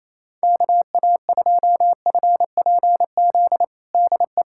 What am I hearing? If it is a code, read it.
Morse code, heard as a keyed tone, about 28 wpm, 700 Hz. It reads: KA2FPZ DE
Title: USA Callsign DE Callsign - 6 Characters - 28wpm